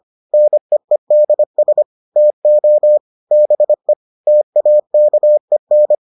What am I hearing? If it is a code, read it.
Morse code, heard as a keyed tone, about 25 wpm, 600 Hz. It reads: NEEDS TO BE TAKEN